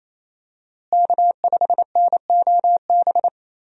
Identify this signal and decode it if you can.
Morse code, heard as a keyed tone, about 27 wpm, 700 Hz. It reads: K5NOB
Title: USA Call Signs - 5 Characters - Repeated 3x - 28wpm